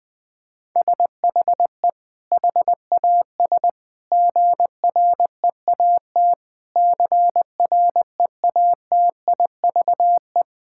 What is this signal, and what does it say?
Morse code, heard as a keyed tone, about 19 wpm, 700 Hz. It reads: SHE HAS GREAT CREATIVE
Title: Sentences from Top 1000 Words - Review Part 6 - 20wpm